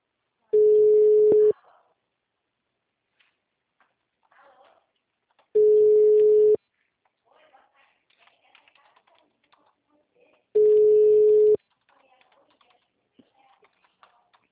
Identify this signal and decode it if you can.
Ringback tone (Europe, 425 Hz), with speech.
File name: out- -2014- - - .83754